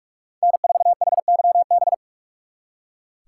Morse code, heard as a keyed tone, about 46 wpm, 700 Hz. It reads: N4HYB
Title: Speed Racing - USA 1x3 Call Signs 45wpm